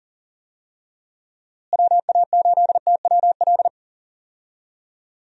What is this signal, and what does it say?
Morse code, heard as a keyed tone, about 40 wpm, 700 Hz. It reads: WA8TWL
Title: USA Call Signs - 6 Characters - Repeated 3x - 40wpm